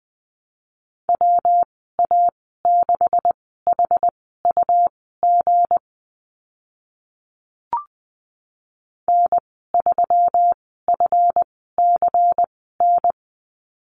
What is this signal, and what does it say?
Morse code, heard as a keyed tone, about 20 wpm, spaced as for 15 wpm, 700 Hz. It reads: WA6HUG N3FCN
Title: 193 Lesson 31g - Letter v - Call Signs 20wpm